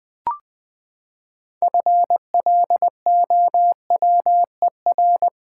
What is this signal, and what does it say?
Morse code, heard as a keyed tone, about 20 wpm, 700 Hz. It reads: FLOWER
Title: Six Letter Words - 20wpm